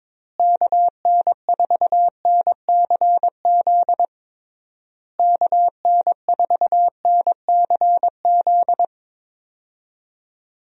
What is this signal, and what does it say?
Morse code, heard as a keyed tone, about 22 wpm, 700 Hz. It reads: KN4NCZ KN4NCZ
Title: USA Call Signs - 6 Characters - Repeated 3x - 22wpm